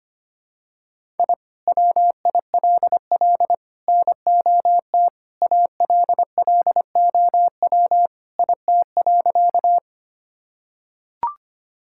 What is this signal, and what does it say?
Morse code, heard as a keyed tone, about 25 wpm, 700 Hz. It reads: I WILL NOT ALLOW IT.
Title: Sentences from Top 1000 Words - Review Part 5 - 25wpm